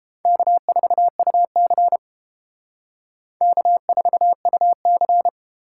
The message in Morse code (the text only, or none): K4UC K4UC